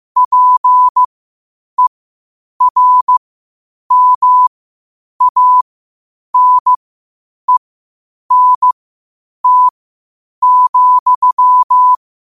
Morse code, heard as a keyed tone, about 15 wpm, spaced as for 8 wpm, 1000 Hz. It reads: PERMANENT,
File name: qotdfiles / 8wpm